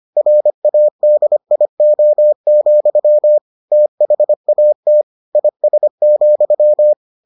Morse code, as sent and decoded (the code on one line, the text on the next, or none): .-. .- -.. .. --- --..-- / - .... .- - / .. ... --..--
RADIO, THAT IS,